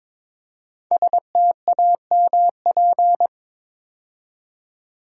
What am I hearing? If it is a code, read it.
Morse code, heard as a keyed tone, about 22 wpm, 700 Hz. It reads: STAMP
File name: 5 Letter Words - First 3 Letters Given - 22wpm